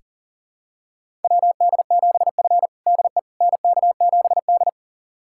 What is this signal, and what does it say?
Morse code, heard as a keyed tone, about 40 wpm, 700 Hz. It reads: WD7F DE NK7D
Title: USA Callsign DE Callsign - 4 Characters - 40wpm